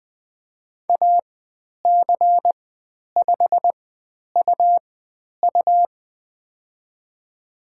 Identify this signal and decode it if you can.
Morse code, heard as a keyed tone, about 20 wpm, spaced as for 10 wpm, 700 Hz. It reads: AC5UU
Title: 109 Lesson 18g - Letter m - Call Signs 20wpm